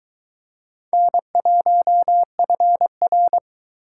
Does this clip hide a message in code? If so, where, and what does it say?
Yes: Morse code from 0.9 to 3.4 s, heard as a keyed tone, about 22 wpm, 700 Hz. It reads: N1FR